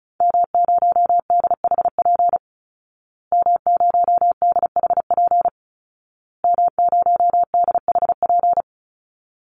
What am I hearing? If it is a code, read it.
Morse code, heard as a keyed tone, about 34 wpm, 700 Hz. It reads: M0DHP M0DHP M0DHP
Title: Call Signs Repeated 2x - 35wpm